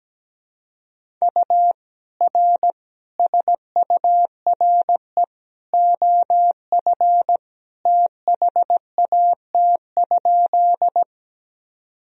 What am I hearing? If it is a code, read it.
Morse code, heard as a keyed tone, about 17 wpm, 700 Hz. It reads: U R SURE OF THAT?
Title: Sentences from Top 300 Words and Abbreviations 17wpm